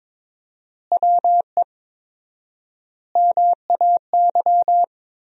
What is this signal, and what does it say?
Morse code, heard as a keyed tone, about 22 wpm, 700 Hz. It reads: WE MAY
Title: 4x Word-Spacing - Sentences from Top 300 Words - 22wpm